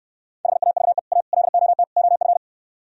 Morse code, heard as a keyed tone, about 66 wpm, 700 Hz. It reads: HAVE A LOT OF